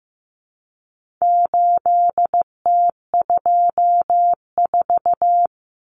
Morse code, heard as a keyed tone, about 15 wpm, 700 Hz. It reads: 8T24
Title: Common Cut Numbers - 4-Digit 15wpm